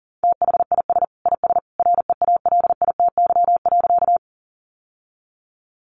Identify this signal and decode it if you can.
Morse code, heard as a keyed tone, about 40 wpm, 700 Hz. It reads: THIS IS REALITY.
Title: Sentences from Top 1200 Words 40wpm